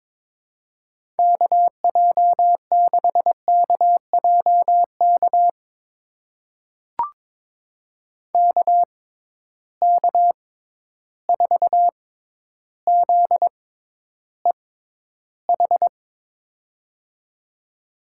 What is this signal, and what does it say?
Morse code, heard as a keyed tone, about 22 wpm, spaced as for 11 wpm, 700 Hz. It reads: KJ6KJK KK4ZEH
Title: USA Call Signs - 6 Characters - 6x Character Spacing - 22wpm